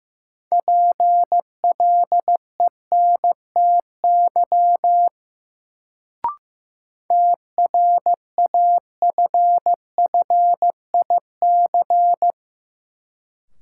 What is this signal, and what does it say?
Morse code, heard as a keyed tone, about 15 wpm, 700 Hz. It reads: PLENTY TRAFFIC
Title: Sentences from Top 1300 Words - New Word Drill - 15wpm